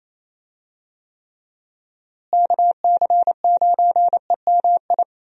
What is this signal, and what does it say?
Morse code, heard as a keyed tone, about 28 wpm, 700 Hz. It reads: KC9EMI
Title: USA Call Signs - 6 Characters - Repeated 2x - 28wpm